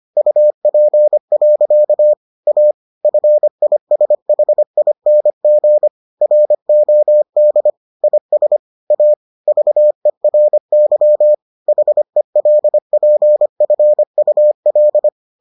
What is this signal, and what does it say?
Morse code, heard as a keyed tone, about 25 wpm, 600 Hz. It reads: UP. A FISHING ROD IS A VERY HELPFUL